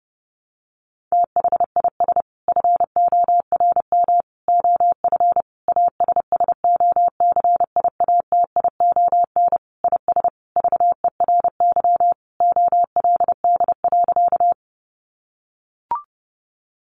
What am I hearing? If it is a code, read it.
Morse code, heard as a keyed tone, about 30 wpm, 700 Hz. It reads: THIS FORM OF ASSOCIATION IS VERY OLD.
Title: Sentences from Top 1100 Words 30wpm